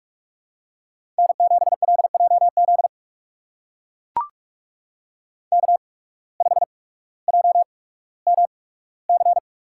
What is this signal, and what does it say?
Morse code, heard as a keyed tone, about 44 wpm, spaced as for 19 wpm, 700 Hz. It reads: N7LJZ K5JMC
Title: USA Call Signs - 5 Characters - 8x Character Spacing - 45wpm